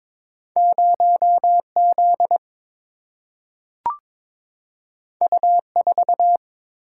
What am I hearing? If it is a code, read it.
Morse code, heard as a keyed tone, about 22 wpm, 700 Hz. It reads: 0 Z U 4
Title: Single-Letter Single-Number 22wpm